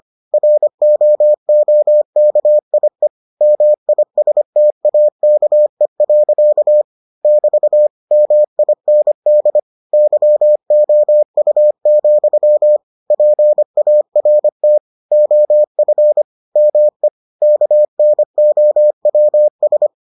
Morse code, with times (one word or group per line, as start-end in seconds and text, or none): 0.3-3.1 ROOKIE
3.4-6.8 MISTAKE.
7.2-7.9 <BT>
8.1-9.6 MIND
9.9-12.8 YOU,
13.1-14.8 PART
15.1-16.2 OF
16.6-17.1 ME
17.4-19.9 KNOWS